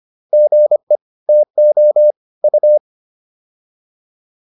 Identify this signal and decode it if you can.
Morse code, heard as a keyed tone, about 25 wpm, 600 Hz. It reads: GE TO U